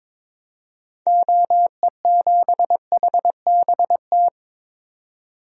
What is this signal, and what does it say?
Morse code, heard as a keyed tone, about 22 wpm, 700 Hz. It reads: OE7HBT
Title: DXCC-100 Call Signs - 6 Characters - 22wpm